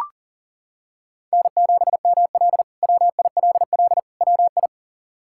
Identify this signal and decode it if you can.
Morse code, heard as a keyed tone, about 40 wpm, 700 Hz. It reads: N7ML WILL WI
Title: K1USN SST QSO Practice - 40wpm